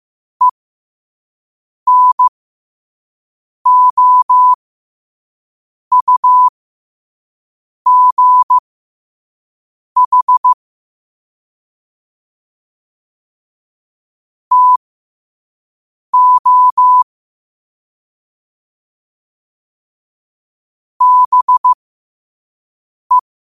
Morse code, heard as a keyed tone, about 15 wpm, spaced as for 5 wpm, 1000 Hz. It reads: ENOUGH TO BE
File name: qotdfiles / 5wpm